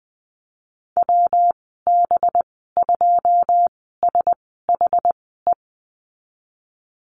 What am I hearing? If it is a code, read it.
Morse code, heard as a keyed tone, about 20 wpm, spaced as for 15 wpm, 700 Hz. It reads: WB2SHE
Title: 186 Lesson 30g - Letter b - Call Signs 20wpm